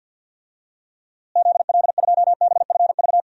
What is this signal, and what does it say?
Morse code, heard as a keyed tone, about 50 wpm, 700 Hz. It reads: ZL2BFV
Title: DXCC-100 Call Signs - 6 Characters - Repeated 3x - 50wpm